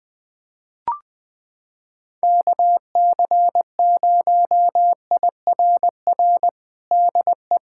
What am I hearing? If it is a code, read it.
Morse code, heard as a keyed tone, about 20 wpm, 700 Hz. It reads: KC0IRR DE
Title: USA Callsign DE Callsign 20wpm